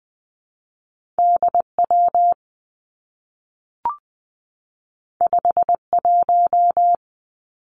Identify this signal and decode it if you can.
Morse code, heard as a keyed tone, about 20 wpm, 700 Hz. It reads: D W 5 1
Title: Two Challenging Letter-Number Combinations 20wpm